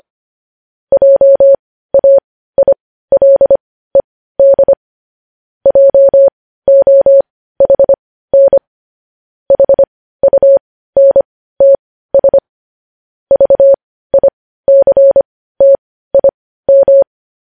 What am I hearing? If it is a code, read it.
Morse code, heard as a keyed tone, about 25 wpm, spaced as for 15 wpm, 550 Hz. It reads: JAILED JOHN HUNTS VICTIM
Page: News Headlines in Morse Code at 15 WPM